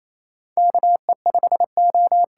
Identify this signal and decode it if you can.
Morse code, heard as a keyed tone, about 28 wpm, 700 Hz. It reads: KE5O
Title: USA Call Signs - 8x Character Spacing - 28wpm